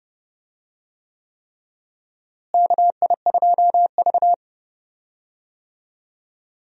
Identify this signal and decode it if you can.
Morse code, heard as a keyed tone, about 31 wpm, 700 Hz. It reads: KI2V
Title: USA Call Signs - 4 Characters - Repeated 3x - 30wpm